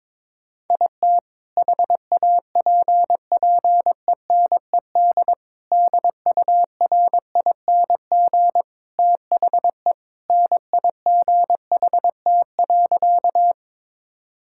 Morse code, heard as a keyed tone, about 21 wpm, 700 Hz. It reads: IT HAPPENED DURING THE NIGHT.
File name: practice / 22wpm / Sentences with 2-Word Phrases - 22wpm